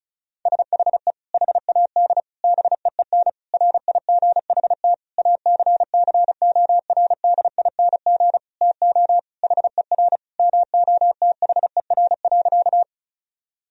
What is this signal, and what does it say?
Morse code, heard as a keyed tone, about 36 wpm, 700 Hz. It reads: SHE HAD BEEN RIGHT ACCORDING TO HER MOTHER.
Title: Sentences from Top 1000 Words - Review Part 4 - 35wpm